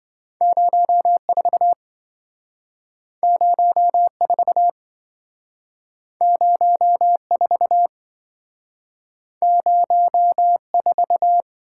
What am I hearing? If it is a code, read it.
Morse code, heard as a keyed tone, about 27 wpm, 700 Hz. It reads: 04 04 04 04